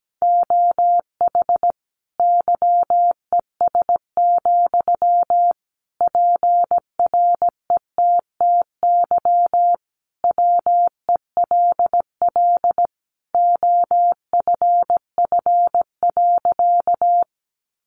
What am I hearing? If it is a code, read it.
Morse code, heard as a keyed tone, about 17 wpm, 700 Hz. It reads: OH YES, PRETTY WELL OFF.
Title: Sentences from Top 1000 Words 17wpm